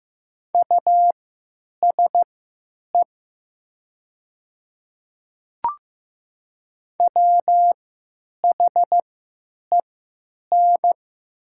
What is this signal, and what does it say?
Morse code, heard as a keyed tone, about 15 wpm, spaced as for 9 wpm, 700 Hz. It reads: USE WHEN
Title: 3x Character Spacing - Top 200 Words - 15wpm